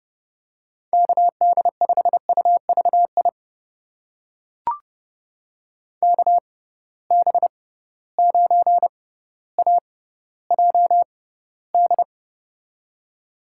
Morse code, heard as a keyed tone, about 30 wpm, spaced as for 15 wpm, 700 Hz. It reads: KD5UVI KB9AJD
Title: USA Call Signs - 6 Characters - 6x Character Spacing - 30wpm